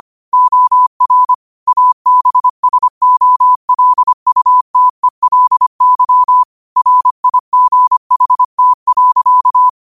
Morse code, heard as a keyed tone, about 25 wpm, 1000 Hz. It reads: OR ABSOLUTELY RIGHT.
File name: qotdfiles / 25wpm